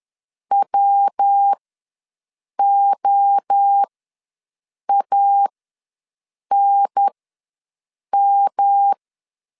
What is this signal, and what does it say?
Morse code, heard as a keyed tone, about 10 wpm, spaced as for 6 wpm, 800 Hz. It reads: WOANM
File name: LEZIONI / 5 LBVC / 5 LB ws 10wpm 3616